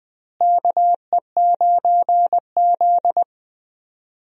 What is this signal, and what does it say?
Morse code, heard as a keyed tone, about 20 wpm, 700 Hz. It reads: KE9Z
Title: USA Call Signs - 4 Characters - Repeated 2x - 20wpm